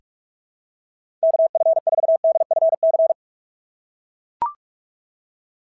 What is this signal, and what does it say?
Morse code, heard as a keyed tone, about 44 wpm, 690 Hz. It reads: KF4DFC
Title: SOTA QSO Practice 45wpm